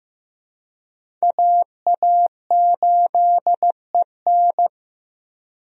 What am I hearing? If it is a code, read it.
Morse code, heard as a keyed tone, about 15 wpm, 700 Hz. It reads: AA8EN
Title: USA Call Signs - 5 Characters - Repeated 2x - 15wpm